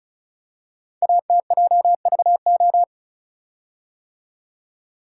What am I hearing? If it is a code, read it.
Morse code, heard as a keyed tone, about 36 wpm, 690 Hz. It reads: ATJVO
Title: Code Groups - 5 Letters - 35wpm